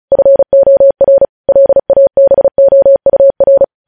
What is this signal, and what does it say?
Morse code, heard as a keyed tone, about 36 wpm, 550 Hz. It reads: FOR LABOUR
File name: podcast / News headlines-1x-35x35